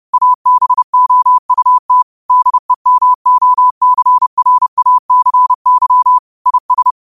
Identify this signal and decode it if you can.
Morse code, heard as a keyed tone, about 30 wpm, 1000 Hz. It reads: ABOUT DEMOCRACY IS
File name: qotdfiles / 30wpm